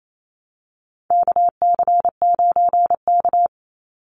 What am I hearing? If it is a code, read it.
Morse code, heard as a keyed tone, about 28 wpm, 700 Hz. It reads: KC9K
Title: Call Signs - 28wpm